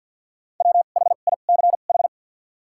Morse code, heard as a keyed tone, about 53 wpm, spaced as for 37 wpm, 700 Hz. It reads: WHICH
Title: 2x Character Spacing - Top 100 Words - 50wpm